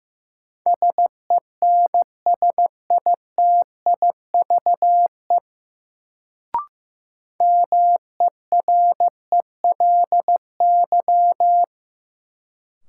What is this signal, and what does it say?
Morse code, heard as a keyed tone, about 15 wpm, 700 Hz. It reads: SENSITIVE MERELY
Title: Sentences from Top 1400 Words - New Word Drill - 15wpm